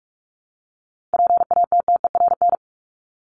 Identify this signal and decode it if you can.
Morse code, heard as a keyed tone, about 44 wpm, 700 Hz. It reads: PATTERN
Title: Top 500 Words Repeated 45wpm